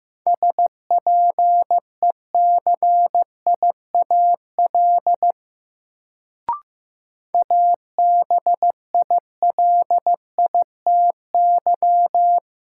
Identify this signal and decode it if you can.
Morse code, heard as a keyed tone, about 15 wpm, 700 Hz. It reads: SPECIAL ABILITY
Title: Sentences from Top 1000 Words - Review Part 3 - New Word Drill - 15wpm